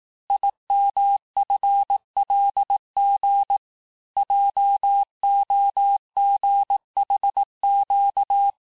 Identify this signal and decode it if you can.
Morse code, heard as a keyed tone, about 18 wpm, 800 Hz. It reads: IMFLG JOGHQ
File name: cours-du-lundi files / cours-1852-texte-chiffre-vitesse-17MPM